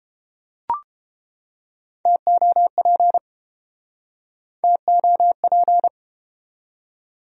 Morse code, heard as a keyed tone, about 31 wpm, 700 Hz. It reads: TOP TOP